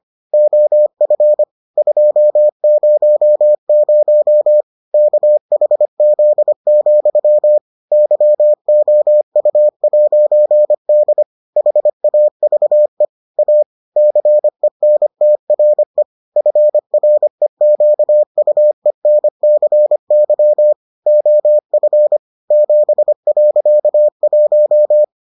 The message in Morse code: OF 200 KHZ, YOU'D HAVE A CENTRE FREQUENCY OF 7.1

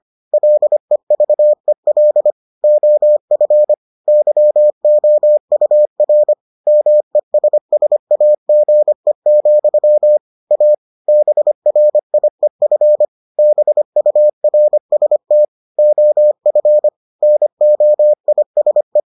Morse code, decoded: LEVEL OF YOUR MESSAGE, A BRIEF BURST OF NOISE

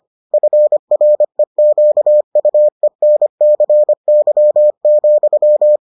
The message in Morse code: FREQUENCY,

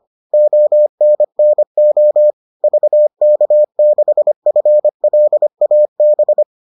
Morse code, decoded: ONNO VK6FLAB

